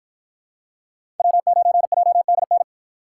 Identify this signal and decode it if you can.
Morse code, heard as a keyed tone, about 53 wpm, 700 Hz. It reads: W9JDN